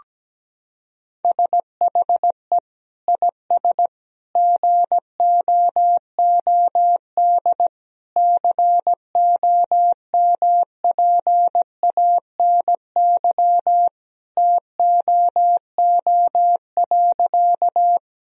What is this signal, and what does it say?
Morse code, heard as a keyed tone, about 17 wpm, 700 Hz. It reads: SHE IS GOOD COMPANY TOO.
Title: Sentences from Top 1000 Words - Review Part 3 - 17wpm